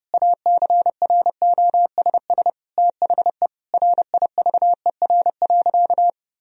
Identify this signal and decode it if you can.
Morse code, heard as a keyed tone, about 31 wpm, 700 Hz. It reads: ACROSS THE RIVER.
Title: Sentences from Top 1000 Words - Review Part 2 - 30wpm